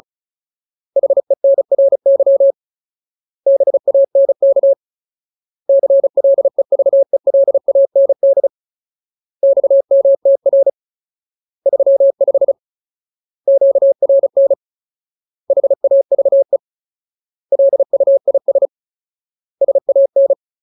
Morse code, heard as a keyed tone, about 35 wpm, 550 Hz. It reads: HENRY BANK CLEVELAND XMTR 35 QRN HAVE LUIS SAN
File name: Ultimate Word List / 35WPM Ultimate